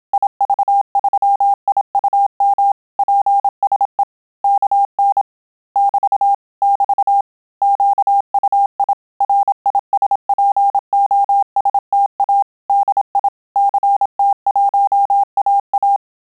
Morse code, heard as a keyed tone, about 27 wpm, 800 Hz. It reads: IV3IUM PSE KN <BT> <BT> QUI RISPOSTA DI CT1AA
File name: Telegraphy-group / QSO CW 25 PPM